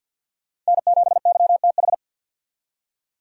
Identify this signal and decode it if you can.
Morse code, heard as a keyed tone, about 62 wpm, 700 Hz. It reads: N7YTH
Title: Speed Racing - USA 1x3 Call Signs 50wpm